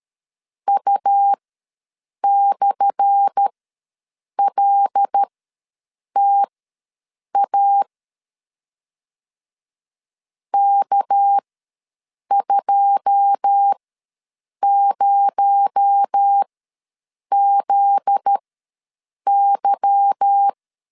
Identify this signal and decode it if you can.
Morse code, heard as a keyed tone, about 13 wpm, spaced as for 7 wpm, 800 Hz. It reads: U/LTA K20ZY
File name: LEZIONI / 10 SIGNS / 10A SLASH ws 12wpm 3443